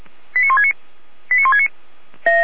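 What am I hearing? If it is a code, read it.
Five-tone selective call (ZVEI), heard as quick runs of five tones, tones of about 70 ms. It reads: 89169 89169
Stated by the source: Five-tone ZVEI selective call received by SDR.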